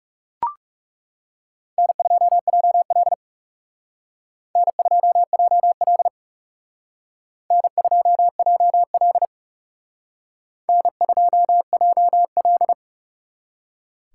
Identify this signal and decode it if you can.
Morse code, heard as a keyed tone, about 38 wpm, 700 Hz. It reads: N2JL N2JL N2JL N2JL
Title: Speed Racing - USA Call Signs 30wpm